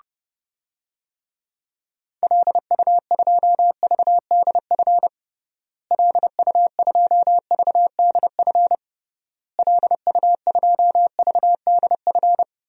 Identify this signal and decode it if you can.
Morse code, heard as a keyed tone, about 30 wpm, 700 Hz. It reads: LU2VDF LU2VDF LU2VDF
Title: DXCC-100 Call Signs - 6 Characters - Repeated 3x - 30wpm